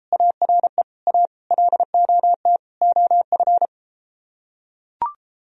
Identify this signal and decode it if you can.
Morse code, heard as a keyed tone, about 33 wpm, 700 Hz. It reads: ARE A LOT OF